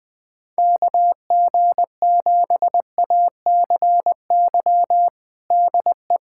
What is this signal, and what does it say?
Morse code, heard as a keyed tone, about 20 wpm, 700 Hz. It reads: KG7ACY DE
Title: USA Callsign DE Callsign - 6 Characters - 20wpm